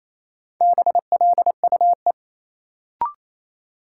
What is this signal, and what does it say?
Morse code, heard as a keyed tone, about 28 wpm, 700 Hz. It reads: BLUE
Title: Sentences from Top 1200 Words - New Word Drill Alt - 28wpm